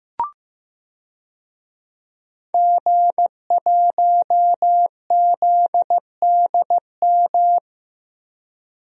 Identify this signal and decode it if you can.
Morse code, heard as a keyed tone, about 15 wpm, 700 Hz. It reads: G1ZDM
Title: DXCC-100 Call Signs - 5 Characters - 15wpm